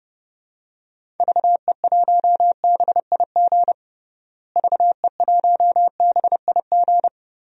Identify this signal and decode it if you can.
Morse code, heard as a keyed tone, about 30 wpm, 700 Hz. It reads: VE1BIG VE1BIG
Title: DXCC-100 Call Signs - 6 Characters - Repeated 3x - 30wpm